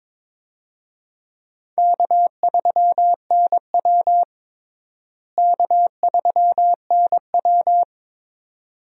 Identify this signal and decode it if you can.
Morse code, heard as a keyed tone, about 22 wpm, 700 Hz. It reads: K3NW K3NW
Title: USA Call Signs - 4 Characters - Repeated 2x - 22wpm